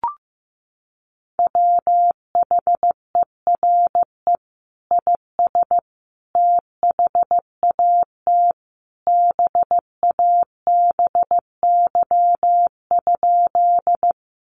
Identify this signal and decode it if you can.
Morse code, heard as a keyed tone, about 15 wpm, 700 Hz. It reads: WHERE IS THAT BABY?
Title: Sentences from Top 1300 Words 15wpm